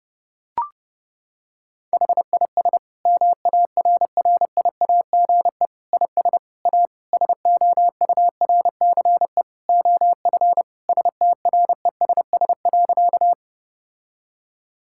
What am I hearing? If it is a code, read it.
Morse code, heard as a keyed tone, about 30 wpm, 700 Hz. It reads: HIS MARRIAGE IS A SOURCE OF STRESS.
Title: Sentences from Top 1000 Words - Review Part 6 - 30wpm